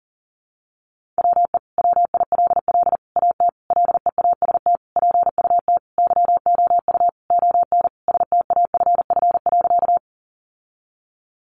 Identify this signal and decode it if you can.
Morse code, heard as a keyed tone, about 40 wpm, 700 Hz. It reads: WE WILL AT LEAST PUT YOU ON STAFF.
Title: Sentences from Top 1200 Words 40wpm